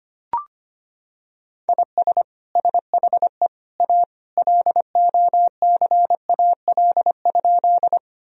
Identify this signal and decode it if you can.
Morse code, heard as a keyed tone, about 25 wpm, 700 Hz. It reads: IS SHE A LOCAL?
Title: Sentences from Top 1000 Words - Review Part 3 - 25wpm